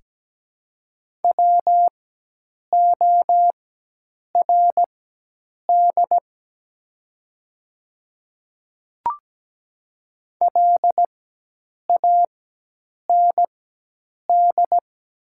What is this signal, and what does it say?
Morse code, heard as a keyed tone, about 17 wpm, spaced as for 8 wpm, 700 Hz. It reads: WORD LAND
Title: 4x Character Spacing - Top 200 Words - 17wpm